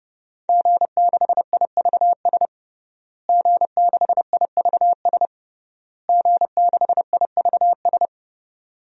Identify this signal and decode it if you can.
Morse code, heard as a keyed tone, about 30 wpm, 700 Hz. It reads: G6IVS G6IVS G6IVS
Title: DXCC-100 Call Signs - 5 Characters - Repeated 3x - 30wpm